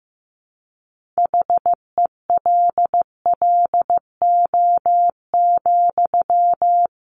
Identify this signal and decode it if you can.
Morse code, heard as a keyed tone, about 15 wpm, 700 Hz. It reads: HELLO,
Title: Sentences from Top 1100 Words 15wpm